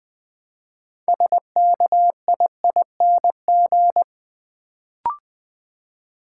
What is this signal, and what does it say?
Morse code, heard as a keyed tone, about 20 wpm, 700 Hz. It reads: SKIING
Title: Build A Word - 2 Letters At A Time - 6-Letter Words - 20wpm